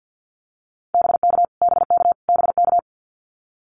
Morse code, heard as a keyed tone, about 49 wpm, 700 Hz. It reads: BK BK BK
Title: Abbreviations Repeated 3x - 50wpm